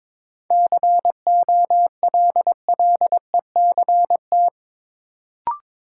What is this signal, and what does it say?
Morse code, heard as a keyed tone, about 21 wpm, 700 Hz. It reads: COLLECT
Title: Sentences from Top 1600 Words - New Word Drill Alt - 22wpm